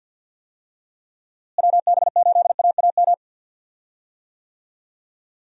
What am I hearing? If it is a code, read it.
Morse code, heard as a keyed tone, about 52 wpm, 700 Hz. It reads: WB8AAM
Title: USA Call Signs - 6 Characters - Repeated 2x - 50wpm